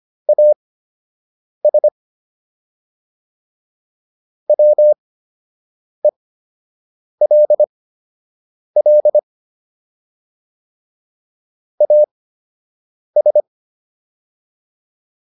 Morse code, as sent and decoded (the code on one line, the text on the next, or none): .- ... / .-- . .-.. .-.. / .- ...
AS WELL AS